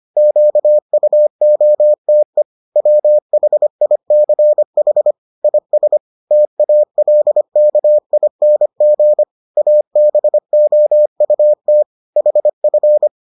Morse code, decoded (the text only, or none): QUOTE WHICH IS TALKING ABOUT HF